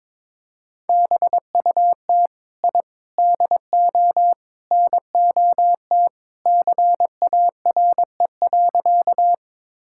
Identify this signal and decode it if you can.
Morse code, heard as a keyed tone, about 22 wpm, 700 Hz. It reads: BUT I DO NOT CARE.
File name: practice / 22wpm / Sentences with 4-Word Phrases - 22wpm